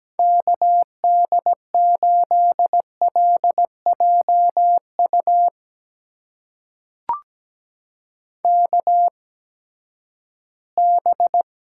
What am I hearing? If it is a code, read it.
Morse code, heard as a keyed tone, about 17 wpm, spaced as for 12 wpm, 700 Hz. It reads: KD8LJU KB
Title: USA Call Signs - 8x Character Spacing - 17wpm